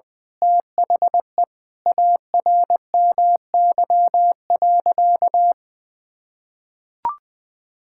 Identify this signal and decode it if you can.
Morse code, heard as a keyed tone, about 21 wpm, 700 Hz. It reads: THE ARMY.
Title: Sentences from Top 1000 Words - Review Part 6 - 20wpm